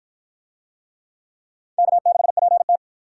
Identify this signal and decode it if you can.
Morse code, heard as a keyed tone, about 53 wpm, 700 Hz. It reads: K6PT